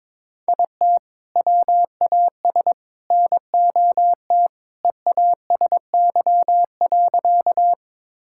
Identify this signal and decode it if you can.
Morse code, heard as a keyed tone, about 22 wpm, 700 Hz. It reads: IT WAS NOT EASY.
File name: practice / 22wpm / Sentences with 3-Word Phrases - 22wpm